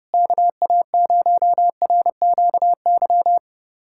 Morse code, heard as a keyed tone, about 30 wpm, 700 Hz. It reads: KA0RQY
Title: USA Callsign DE Callsign - 6 Characters - 30wpm